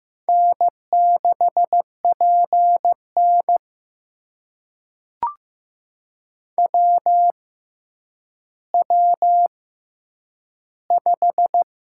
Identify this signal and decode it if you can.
Morse code, heard as a keyed tone, about 15 wpm, spaced as for 9 wpm, 700 Hz. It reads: N6PN WW5
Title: USA Call Signs - 4 Characters - 6x Character Spacing - 15wpm